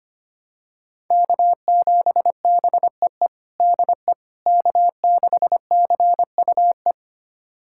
Morse code, heard as a keyed tone, about 25 wpm, 700 Hz. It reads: K7BEE DE K6CUE
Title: USA Callsign DE Callsign - 5 Characters - 25wpm